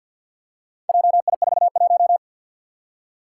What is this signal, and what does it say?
Morse code, heard as a keyed tone, about 52 wpm, 700 Hz. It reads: JI41
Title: Four Challenging Letter-Number Combinations 50wpm